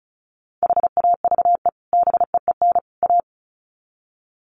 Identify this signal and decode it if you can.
Morse code, heard as a keyed tone, about 34 wpm, 700 Hz. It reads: HAVE BEEN A
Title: Sets of 3 Words 35wpm